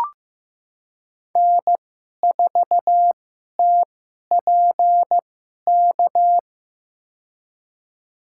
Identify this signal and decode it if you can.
Morse code, heard as a keyed tone, about 15 wpm, spaced as for 11 wpm, 700 Hz. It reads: N4TPK
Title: USA Call Signs - 2x Character Spacing - 15wpm